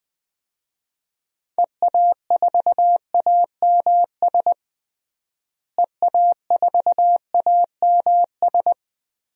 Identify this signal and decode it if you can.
Morse code, heard as a keyed tone, about 20 wpm, 700 Hz. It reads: EA4AMS EA4AMS
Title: DXCC-100 Call Signs - 6 Characters - Repeated 3x - 20wpm